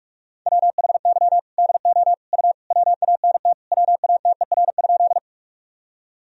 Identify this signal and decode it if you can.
Morse code, heard as a keyed tone, about 46 wpm, 700 Hz. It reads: WHY DO U WANT WATER?
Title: Sentences from Top 200 Words and Abbreviations 45wpm